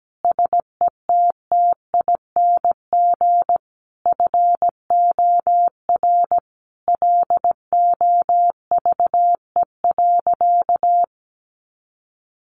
Morse code, heard as a keyed tone, about 17 wpm, 700 Hz. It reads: SETTING FOR LOVE.
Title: Sentences from Top 1300 Words 17wpm